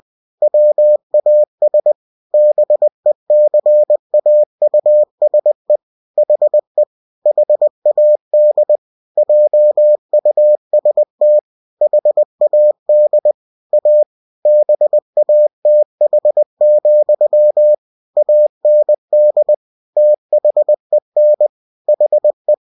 Morse code, heard as a keyed tone, about 20 wpm, 600 Hz. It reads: WAS BECAUSE HE HAD JUST HAD A BATH, AND THEN HE